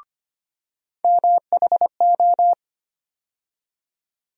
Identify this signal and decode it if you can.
Morse code, heard as a keyed tone, about 25 wpm, 700 Hz. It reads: MHO
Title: USA Call Sign Postfixes - 3 Letters 25wpm